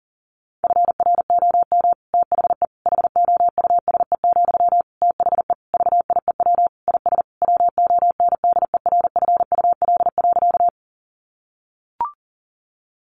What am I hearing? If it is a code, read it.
Morse code, heard as a keyed tone, about 40 wpm, 700 Hz. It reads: FROM THE HOUSE, THE VIEW IS WONDERFUL.
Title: Sentences from Top 1100 Words 40wpm